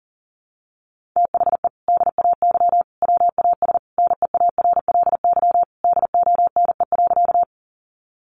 Morse code, heard as a keyed tone, about 40 wpm, 700 Hz. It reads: THE DAY WAS NEARLY DONE.
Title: Sentences from Top 1300 Words 40wpm